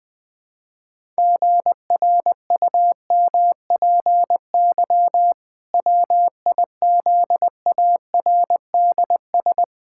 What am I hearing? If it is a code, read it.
Morse code, heard as a keyed tone, about 20 wpm, 700 Hz. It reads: GRUMPY WIZARDS
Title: Pangram Challenge v2 - 20wpm